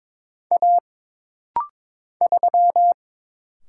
Morse code, heard as a keyed tone, about 22 wpm, 700 Hz. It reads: A 3
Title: Single Letter-Number Alt - Rapid-Fire 22wpm